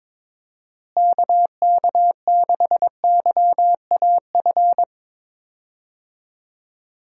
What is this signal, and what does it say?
Morse code, heard as a keyed tone, about 21 wpm, 700 Hz. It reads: KK6YAF
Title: USA Call Signs - 6 Characters - Repeated 3x - 22wpm